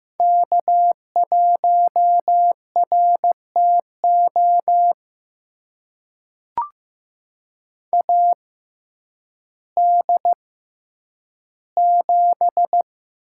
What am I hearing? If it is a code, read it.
Morse code, heard as a keyed tone, about 15 wpm, spaced as for 9 wpm, 700 Hz. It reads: K1RTO AD7
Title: USA Call Signs - 5 Characters - 6x Character Spacing - 15wpm